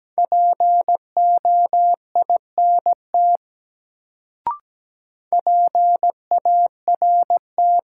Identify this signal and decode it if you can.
Morse code, heard as a keyed tone, about 17 wpm, 700 Hz. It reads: POINT PART
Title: Sentences from Top 200 Words - New Word Drill - 17wpm